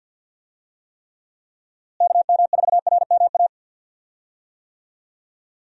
Morse code, heard as a keyed tone, about 51 wpm, 700 Hz. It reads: KM4RMA
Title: USA Call Signs - 6 Characters - Repeated 2x - 50wpm